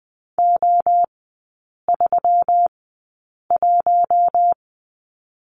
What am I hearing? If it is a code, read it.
Morse code, heard as a keyed tone, about 20 wpm, spaced as for 8 wpm, 700 Hz. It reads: O31
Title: 121 Lesson 20e - Number 3 - Three Characters 20wpm